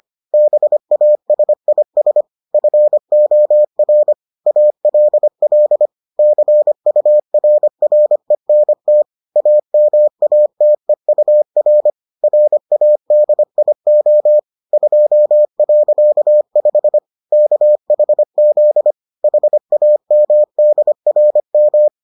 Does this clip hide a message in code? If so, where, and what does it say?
Yes: Morse code from 0.3 to 21.9 s, heard as a keyed tone, about 25 wpm, 600 Hz. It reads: BASIS FOR ALL CURRENT AMATEUR RADIO 2.5 KHZ HAMDRM